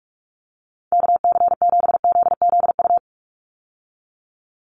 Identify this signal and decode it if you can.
Morse code, heard as a keyed tone, about 44 wpm, 700 Hz. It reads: KC7ZZU
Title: USA Call Signs 45wpm